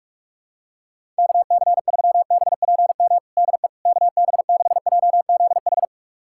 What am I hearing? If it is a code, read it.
Morse code, heard as a keyed tone, about 41 wpm, 700 Hz. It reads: KC3BPM DE KB6JZH
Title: USA Callsign DE Callsign - 6 Characters - 45wpm